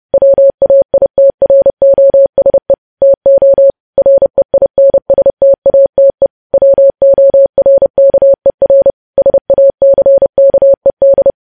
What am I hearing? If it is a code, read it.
Morse code, heard as a keyed tone, about 30 wpm, 550 Hz. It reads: WAITROSE TO REINSTATE WORKER SACKED